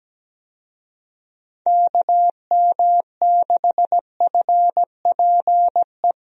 Morse code, heard as a keyed tone, about 17 wpm, 700 Hz. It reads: KM6FPE
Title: USA Call Signs - 6 Characters - Repeated 2x - 17wpm